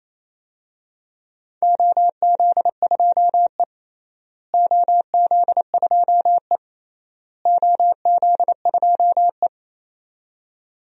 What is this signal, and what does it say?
Morse code, heard as a keyed tone, about 28 wpm, 700 Hz. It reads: OZ2E OZ2E OZ2E
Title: DXCC-100 Call Signs - 4 Characters - Repeated 3x - 28wpm